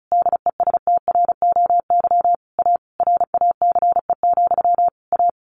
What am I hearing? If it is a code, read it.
Morse code, heard as a keyed tone, about 34 wpm, 700 Hz. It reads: DESTROY A RACE, A
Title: A Sound Of Thunder 35wpm